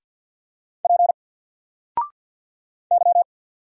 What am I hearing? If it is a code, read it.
Morse code, heard as a keyed tone, about 47 wpm, 700 Hz. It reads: P Y
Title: Single Letter-Number 50wpm